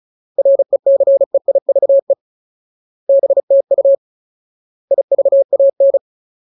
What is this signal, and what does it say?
Morse code, heard as a keyed tone, about 34 wpm, 550 Hz. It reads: RECEIVE BTU IVAN
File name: Ultimate Word List / 35WPM Ultimate